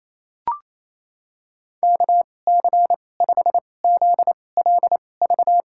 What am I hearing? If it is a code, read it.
Morse code, heard as a keyed tone, about 28 wpm, spaced as for 20 wpm, 700 Hz. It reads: KC5ZLV
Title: USA Call Signs - 6 Characters - 2x Character Spacing - 28wpm